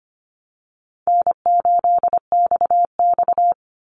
Morse code, heard as a keyed tone, about 25 wpm, 700 Hz. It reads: N8XX
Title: Easy Call Signs - 25wpm